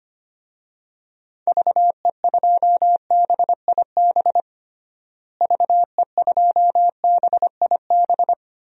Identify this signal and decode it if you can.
Morse code, heard as a keyed tone, about 25 wpm, 700 Hz. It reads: VE2BIB VE2BIB
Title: DXCC-100 Call Signs - 6 Characters - Repeated 2x - 25wpm